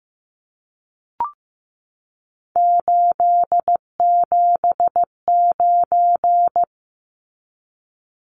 Morse code, heard as a keyed tone, about 15 wpm, 700 Hz. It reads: 879
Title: Numbers - 3-Digit 15wpm